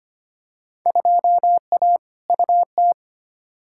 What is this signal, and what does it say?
Morse code, heard as a keyed tone, about 25 wpm, 700 Hz. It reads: 2A UT
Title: Field Day Exchanges - 25wpm